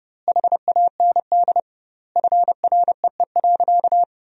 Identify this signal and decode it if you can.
Morse code, heard as a keyed tone, about 32 wpm, 700 Hz. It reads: HAND FREE.
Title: 2x Word-Spacing - Sentences from Top 500 Words - 30wpm